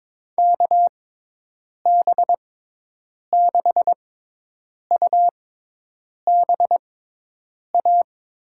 Morse code, heard as a keyed tone, about 22 wpm, spaced as for 8 wpm, 700 Hz. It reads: KB6UBA
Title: USA Call Signs - 6 Characters - 6x Character Spacing - 22wpm